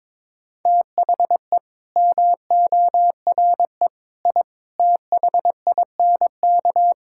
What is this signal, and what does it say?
Morse code, heard as a keyed tone, about 22 wpm, 700 Hz. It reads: THE MORE I THINK
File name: practice / 22wpm / Sets of 8 Words - 22wpm